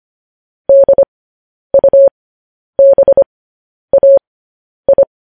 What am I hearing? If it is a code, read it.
Morse code, heard as a keyed tone, about 25 wpm, spaced as for 10 wpm, 550 Hz. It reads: DUBAI